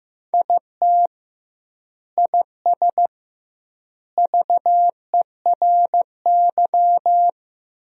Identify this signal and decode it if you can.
Morse code, heard as a keyed tone, about 15 wpm, 700 Hz. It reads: IT IS VERY
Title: 2x Word-Spacing - Sentences from Top 200 Words - 15wpm